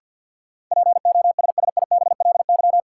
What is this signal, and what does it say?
Morse code, heard as a keyed tone, about 53 wpm, 700 Hz. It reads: POSSIBLY
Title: Sentences from Top 1300 Words - New Word Drill - 50wpm